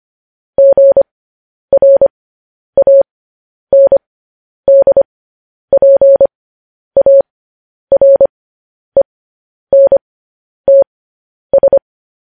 Morse code, heard as a keyed tone, about 25 wpm, spaced as for 10 wpm, 550 Hz. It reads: GRANDPARENTS
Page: news-headlines-in-morse-code-at-10-wpm